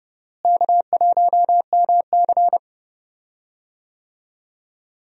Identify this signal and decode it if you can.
Morse code, heard as a keyed tone, about 29 wpm, 700 Hz. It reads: K1MC
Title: USA Call Signs - 4 Characters - Repeated 3x - 30wpm